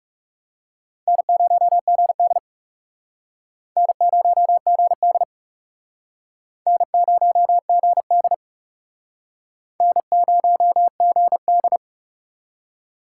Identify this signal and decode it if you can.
Morse code, heard as a keyed tone, about 36 wpm, 700 Hz. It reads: N0GD N0GD N0GD N0GD
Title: Speed Racing - USA 1x2 Call Signs 30wpm